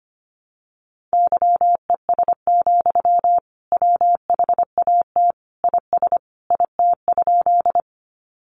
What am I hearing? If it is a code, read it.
Morse code, heard as a keyed tone, about 25 wpm, 700 Hz. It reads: YES, WHAT IS IT?
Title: Sentences from Top 500 Words 25wpm